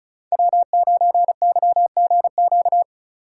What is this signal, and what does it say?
Morse code, heard as a keyed tone, about 47 wpm, spaced as for 41 wpm, 700 Hz. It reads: W9YGQ
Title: Speed Racing - USA 1x3 Call Signs 35wpm